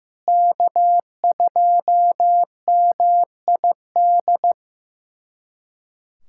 Morse code, heard as a keyed tone, about 15 wpm, 700 Hz. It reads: K2MID
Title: USA Callsign DE Callsign - 5 Characters - 15wpm